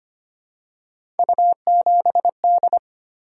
Morse code, heard as a keyed tone, about 25 wpm, 700 Hz. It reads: U7D
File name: Three Letter-Number Combinations - 25wpm